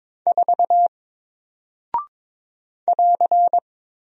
Morse code, heard as a keyed tone, about 22 wpm, 700 Hz. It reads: 4 <AR>
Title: Single Character (Letter:Number:Punctuation:Prosign) Alt 22wpm